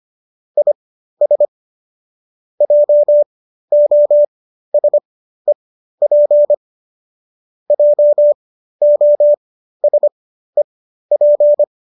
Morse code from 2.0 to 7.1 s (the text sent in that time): JOSEP